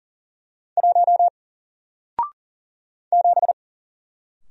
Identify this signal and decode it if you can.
Morse code, heard as a keyed tone, about 40 wpm, 700 Hz. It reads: 1 7
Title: Single Letter-Number Alt 40wpm